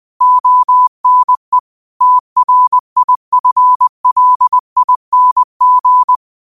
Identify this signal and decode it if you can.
Morse code, heard as a keyed tone, about 20 wpm, 1000 Hz. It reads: ONE TRIFLING
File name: qotdfiles / 20wpm